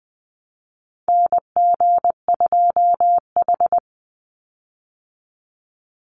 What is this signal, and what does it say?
Morse code, heard as a keyed tone, about 20 wpm, 700 Hz. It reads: NG2H
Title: Call Signs Repeated 3x - 20wpm